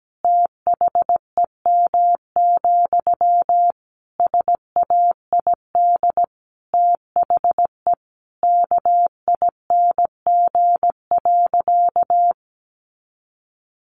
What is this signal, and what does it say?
Morse code, heard as a keyed tone, about 17 wpm, 700 Hz. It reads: THEM, SAID THE KING.
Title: Sentences from Top 400 Words 17wpm